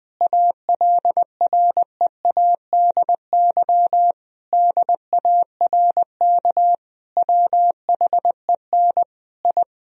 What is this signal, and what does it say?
Morse code, heard as a keyed tone, about 19 wpm, 700 Hz. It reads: ALREADY DARK WHEN I
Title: Sentences from Top 1000 Words - Review Part 2 - 20wpm